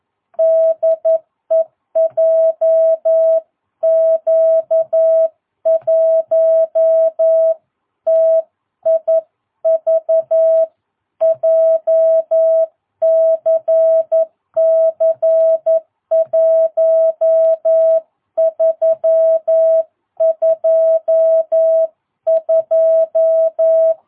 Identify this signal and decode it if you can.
Morse code, heard as a keyed tone, about 11 wpm, 660 Hz. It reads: DE JQ1TIV JCC 1322
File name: jh8cbh / 2022cw